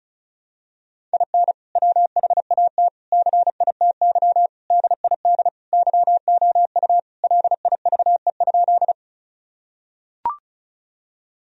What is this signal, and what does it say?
Morse code, heard as a keyed tone, about 36 wpm, 700 Hz. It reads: IN WHAT CITY DID YOU LIVE?
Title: Sentences from Top 1000 Words - Review Part 1 - 35wpm